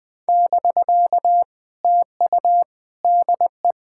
Morse code, heard as a keyed tone, about 20 wpm, 700 Hz. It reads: <BK> TU DE
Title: POTA QSO Practice - 20wpm